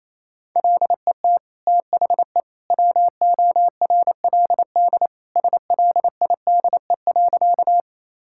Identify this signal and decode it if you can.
Morse code, heard as a keyed tone, about 28 wpm, 700 Hz. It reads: LET THE WORLD SLIDE.
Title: Sentences from Top 1600 Words 28wpm